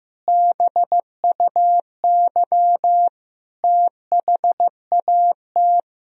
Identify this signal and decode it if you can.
Morse code, heard as a keyed tone, about 15 wpm, 700 Hz. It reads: BUY THAT
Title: Sentences from Top 1000 Words - Review Part 4 - 15wpm